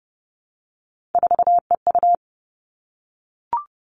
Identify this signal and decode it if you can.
Morse code, heard as a keyed tone, about 30 wpm, 700 Hz. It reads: 4EU
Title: All Cut Numbers - 3-Digit 30wpm